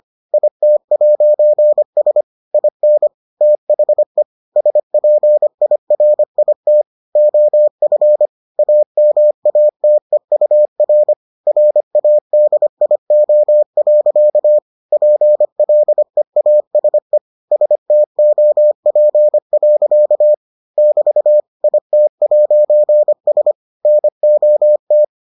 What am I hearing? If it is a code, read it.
Morse code, heard as a keyed tone, about 25 wpm, 600 Hz. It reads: IT'S IN THE SPIRIT OF AMATEUR RADIO. PLEASE STOP. <BT> IT'S NOT